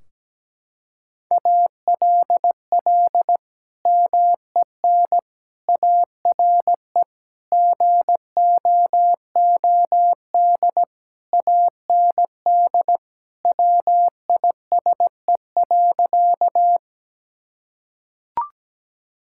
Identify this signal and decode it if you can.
Morse code, heard as a keyed tone, about 17 wpm, 700 Hz. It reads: ALL MEN ARE GOOD AND WISE.
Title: Sentences from Top 1600 Words 17wpm